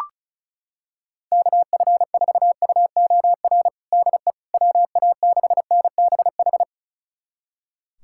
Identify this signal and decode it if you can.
Morse code, heard as a keyed tone, about 36 wpm, 700 Hz. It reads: KF4UOR DE WA6NBH
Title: USA Callsign DE Callsign - 6 Characters - 35wpm